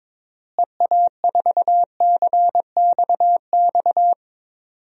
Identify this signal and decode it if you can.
Morse code, heard as a keyed tone, about 22 wpm, 700 Hz. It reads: EA4CXX
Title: DXCC-100 Call Signs - 6 Characters - Repeated 3x - 22wpm